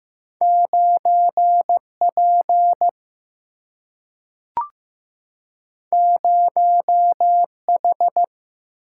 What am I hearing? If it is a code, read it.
Morse code, heard as a keyed tone, about 15 wpm, 700 Hz. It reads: 9P 0H
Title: Single-Letter Single-Number 15wpm